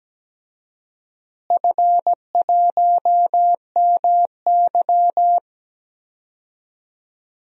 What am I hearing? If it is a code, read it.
Morse code, heard as a keyed tone, about 17 wpm, 700 Hz. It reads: F1MY
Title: DXCC-100 Call Signs - 4 Characters - Repeated 2x - 17wpm